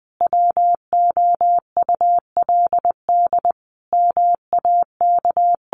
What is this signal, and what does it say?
Morse code, heard as a keyed tone, about 20 wpm, 700 Hz. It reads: WOULD MAK
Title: Sentences from Top 1300 Words 20wpm